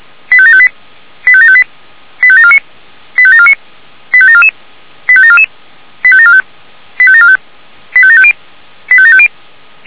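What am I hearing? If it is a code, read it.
Five-tone selective call (ZVEI), heard as quick runs of five tones, tones of about 70 ms. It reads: 85758 85758 85749 85749 85744 85744 85745 85745 85750 85750
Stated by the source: ZVEI five-tone selective call received by SDR.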